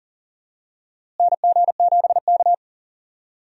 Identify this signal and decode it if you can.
Morse code, heard as a keyed tone, about 40 wpm, 700 Hz. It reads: NG7K
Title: USA Call Signs - 4 Characters - Repeated 2x - 40wpm